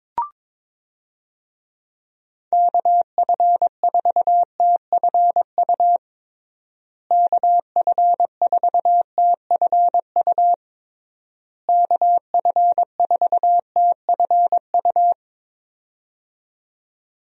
Morse code, heard as a keyed tone, about 22 wpm, 700 Hz. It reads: KF4TFU KF4TFU KF4TFU
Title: USA Call Signs - 6 Characters - Repeated 3x - 22wpm